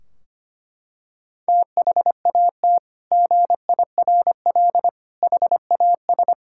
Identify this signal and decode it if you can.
Morse code, heard as a keyed tone, about 25 wpm, 700 Hz. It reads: THAT GIRL HAS
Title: Sentences from Top 1600 Words 25wpm